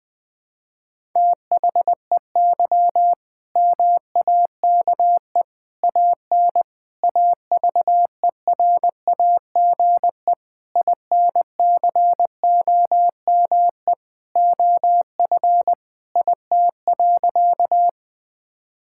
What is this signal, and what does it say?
Morse code, heard as a keyed tone, about 21 wpm, 700 Hz. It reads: THEY MAKE AN AVERAGE INCOME OF IT.
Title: Sentences from Top 1000 Words - Review Part 6 - 20wpm